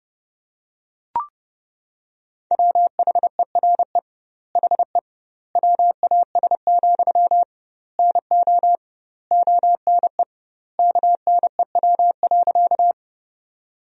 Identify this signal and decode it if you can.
Morse code, heard as a keyed tone, about 30 wpm, 700 Hz. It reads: WHERE HE WAS, NO ONE KNEW.
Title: 2x Word-Spacing - Sentences from Top 400 Words - 30wpm